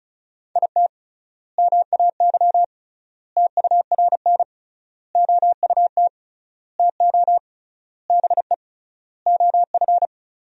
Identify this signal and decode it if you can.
Morse code, heard as a keyed tone, about 34 wpm, 700 Hz. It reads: IT MAY TURN OUT TO BE OF
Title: 3x Word-Spacing - Sentences from Top 400 Words - 35wpm